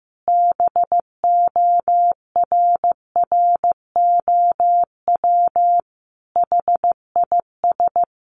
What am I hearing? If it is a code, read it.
Morse code, heard as a keyed tone, about 15 wpm, 700 Hz. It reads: BORROW HIS
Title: Additional Easy Phrases 15wpm